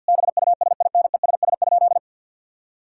Morse code, heard as a keyed tone, about 50 wpm, 700 Hz. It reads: BUSINESS?
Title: Sentences from Top 1000 Words - Review Part 6 - 50wpm